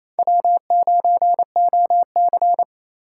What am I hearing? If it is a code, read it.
Morse code, heard as a keyed tone, about 28 wpm, 700 Hz. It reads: W9OC
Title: USA Call Signs - 4 Characters - 2x Character Spacing - 28wpm